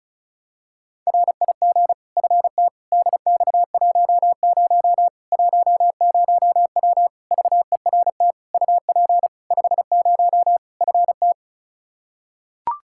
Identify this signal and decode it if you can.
Morse code, heard as a keyed tone, about 36 wpm, 700 Hz. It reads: RIG FT DX10 10W VERT UP 50 FT
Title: Practice QSO - Exchange 2 - Info - 35wpm